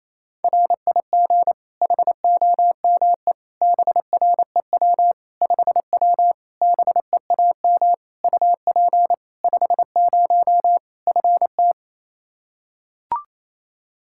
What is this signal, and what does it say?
Morse code, heard as a keyed tone, about 28 wpm, 700 Hz. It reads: RIG HOME BREW 5W BEAM UP 50 FT
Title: Practice QSO - Exchange 2 - Info - 28wpm